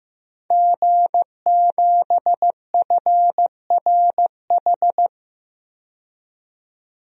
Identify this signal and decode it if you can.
Morse code, heard as a keyed tone, about 15 wpm, 700 Hz. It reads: G7FRH
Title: DXCC-100 Call Signs - 5 Characters - 15wpm